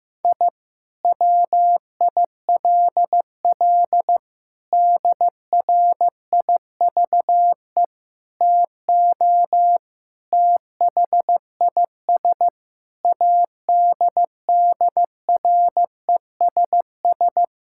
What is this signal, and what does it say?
Morse code, heard as a keyed tone, about 15 wpm, 700 Hz. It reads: I WILL DRIVE TO THIS ADDRESS
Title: Sentences from Top 1000 Words - Review Part 6 - 15wpm